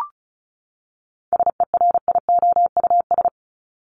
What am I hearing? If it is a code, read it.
Morse code, heard as a keyed tone, about 34 wpm, 700 Hz. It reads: SERIOUS
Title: LY Suffix - Top 100 Words with Base Words 35wpm